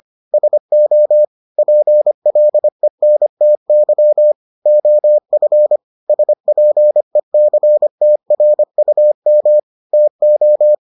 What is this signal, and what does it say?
Morse code, heard as a keyed tone, about 25 wpm, 600 Hz. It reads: SO PLENTY OF SPECTRUM TO